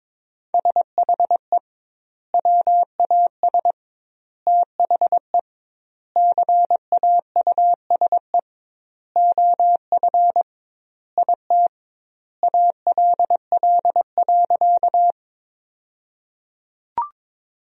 Morse code, heard as a keyed tone, about 22 wpm, 700 Hz. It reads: SHE WAS THE CAUSE OF IT ALL.
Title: 2x Word-Spacing - Sentences from Top 200 Words - 22wpm